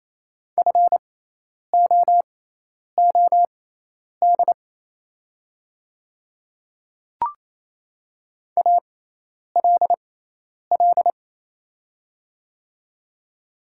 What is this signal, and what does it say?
Morse code, heard as a keyed tone, about 27 wpm, spaced as for 10 wpm, 700 Hz. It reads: FOOD ALL
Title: 6x Character Spacing - Top 500 Words - 28wpm